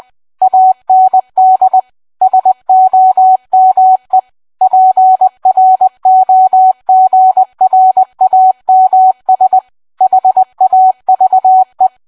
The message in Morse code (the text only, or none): AND SOME PROGRAMS HAVE